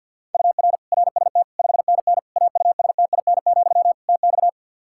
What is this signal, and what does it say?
Morse code, heard as a keyed tone, about 50 wpm, 700 Hz. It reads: UR RST 5NN AUSTIN, TX
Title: Practice QSO - Exchange 1 - Beginning - 50wpm